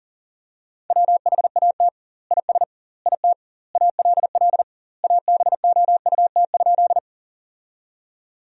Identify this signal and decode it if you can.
Morse code, heard as a keyed tone, about 40 wpm, 700 Hz. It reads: WHAT IS IT ALL ABOUT?
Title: 2x Word-Spacing - Sentences from Top 100 Words - 40wpm